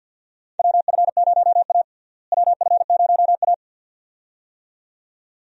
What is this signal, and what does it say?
Morse code, heard as a keyed tone, about 47 wpm, 700 Hz. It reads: WF0A WF0A
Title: USA Call Signs - 4 Characters - Repeated 2x - 50wpm